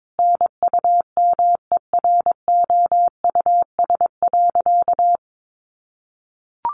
Morse code, heard as a keyed tone, about 22 wpm, 700 Hz. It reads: NUMEROUS.
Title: Sentences from Top 1400 Words 22wpm